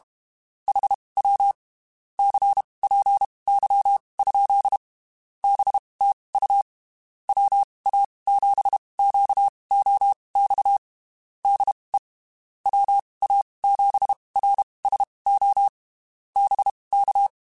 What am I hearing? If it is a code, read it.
Morse code, heard as a keyed tone, about 32 wpm, spaced as for 23 wpm, 800 Hz. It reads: HW CPY? BTU WA7QOX DE WA7RSO BK
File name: (spaced) / QSOs Spaced-mp3s / WA7QOX 1 @30 sp